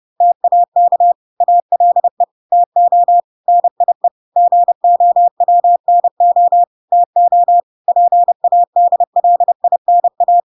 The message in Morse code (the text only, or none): TAK ALE TO NIE GOWNO TO PADLINA